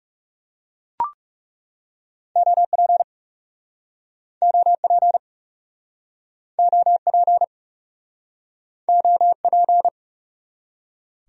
Morse code, heard as a keyed tone, about 40 wpm, 700 Hz. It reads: OP OP OP OP